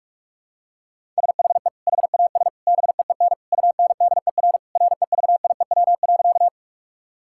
Morse code, heard as a keyed tone, about 42 wpm, 700 Hz. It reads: SHE HAS BEEN UNDER REVIEW.
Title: Sentences from Top 1000 Words - Review Part 6 - 45wpm